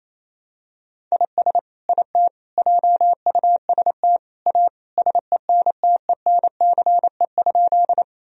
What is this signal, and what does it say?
Morse code, heard as a keyed tone, about 28 wpm, 700 Hz. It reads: IS IT JUST A SENTENCE?
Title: Sentences from Top 1000 Words - Review Part 1 - 28wpm